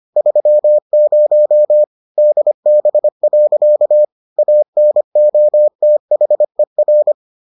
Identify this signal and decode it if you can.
Morse code, heard as a keyed tone, about 25 wpm, 600 Hz. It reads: 30 DB. ANOTHER